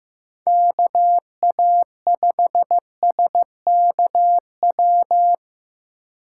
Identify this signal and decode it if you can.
Morse code, heard as a keyed tone, about 15 wpm, 700 Hz. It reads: KA5SKW
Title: USA Call Signs - 6 Characters - 8x Character Spacing - 15wpm